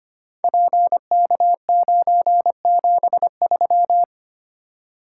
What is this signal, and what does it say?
Morse code, heard as a keyed tone, about 25 wpm, 700 Hz. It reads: PK973
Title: Code Groups - 5 Letter:Numbers - 25wpm